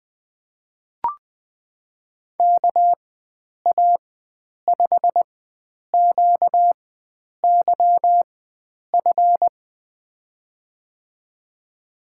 Morse code, heard as a keyed tone, about 21 wpm, spaced as for 9 wpm, 700 Hz. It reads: KA5QYF
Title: USA Call Signs - 6 Characters - 4x Character Spacing - 20wpm